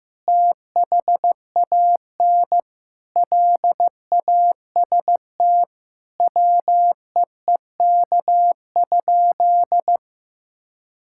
Morse code, heard as a keyed tone, about 16 wpm, 700 Hz. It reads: THAN LAST WEEK?
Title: Sentences from Top 1000 Words - Review Part 5 - 15wpm